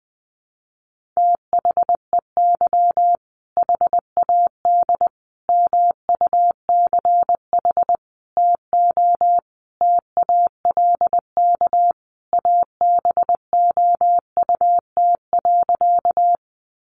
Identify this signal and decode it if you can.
Morse code, heard as a keyed tone, about 20 wpm, 700 Hz. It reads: THEY HAD MUCH TO TALK ABOUT.
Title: Sentences from Top 400 Words 20wpm